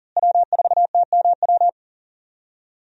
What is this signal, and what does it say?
Morse code, heard as a keyed tone, about 39 wpm, 700 Hz. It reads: W4TMW
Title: USA Call Signs - 5 Characters - Repeated 2x - 40wpm